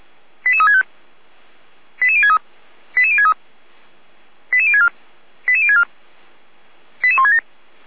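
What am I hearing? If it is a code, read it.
Five-tone selective call (ZVEI), heard as quick runs of five tones, tones of about 70 ms. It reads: 88365 88063 88063 88064 88064 88167
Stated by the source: ZVEI five-tone selective call received by SDR.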